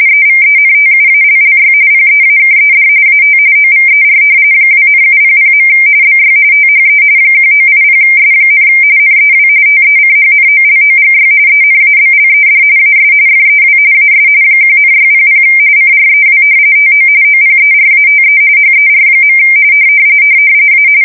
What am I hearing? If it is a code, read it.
RTTY, about 45.45 baud. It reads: EVENTUALLY PEAKED AT OVER 2,700. BY THE 1980S, NEW TECHNOLOGY ENABLED AS FEW AS 1,600 CREW TO OPERATE HER. A
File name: fsk170 / battleshipnj